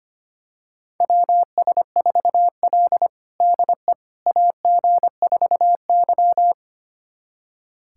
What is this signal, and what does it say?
Morse code, heard as a keyed tone, about 25 wpm, 700 Hz. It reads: WS4L DE AG4Y
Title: USA Callsign DE Callsign - 4 Characters - 25wpm